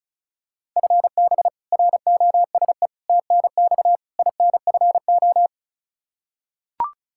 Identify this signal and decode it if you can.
Morse code, heard as a keyed tone, about 36 wpm, 700 Hz. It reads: FB ROSE TNX INFO
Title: Practice QSO - Exchange 3 - More Info - 35wpm